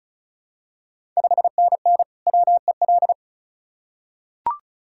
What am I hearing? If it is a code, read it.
Morse code, heard as a keyed tone, about 36 wpm, 690 Hz. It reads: 5NN WEL
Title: Ontario QSO Party Exchange 35wpm